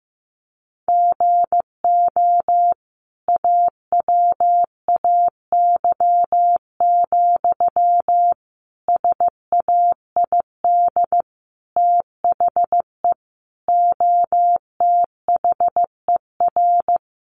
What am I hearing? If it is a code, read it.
Morse code, heard as a keyed tone, about 15 wpm, 700 Hz. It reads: GO AWAY, SAID THE OTHER
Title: Sentences from Top 1500 Words 15wpm